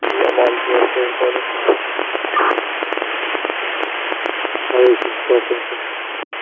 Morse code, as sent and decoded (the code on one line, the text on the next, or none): none